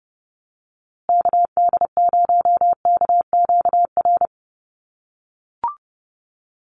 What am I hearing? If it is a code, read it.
Morse code, heard as a keyed tone, about 30 wpm, 700 Hz. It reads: KD0KQR
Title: USA Call Signs Repeated 2x 30wpm